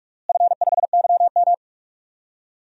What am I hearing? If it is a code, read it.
Morse code, heard as a keyed tone, about 46 wpm, 700 Hz. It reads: F5YM
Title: DXCC-100 Call Signs - 4 Characters - Repeated 2x - 45wpm